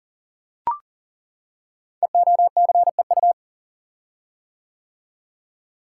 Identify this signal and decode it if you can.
Morse code, heard as a keyed tone, about 38 wpm, 700 Hz. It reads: EOCEU
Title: Code Groups - 5 Letter:Numbers - 40wpm